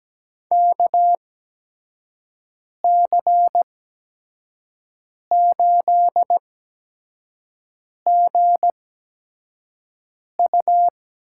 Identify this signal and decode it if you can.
Morse code, heard as a keyed tone, about 17 wpm, spaced as for 5 wpm, 700 Hz. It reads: KC8GU
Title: USA Call Signs - 6 Characters - 8x Character Spacing - 17wpm